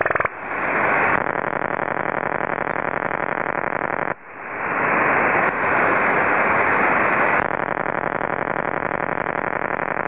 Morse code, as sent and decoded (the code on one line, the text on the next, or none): none